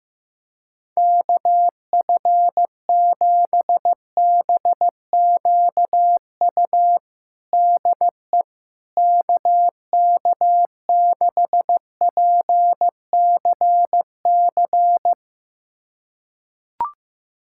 Morse code, heard as a keyed tone, about 15 wpm, 700 Hz. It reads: KF7BQU DE KK6PCC
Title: USA Callsign DE Callsign 15wpm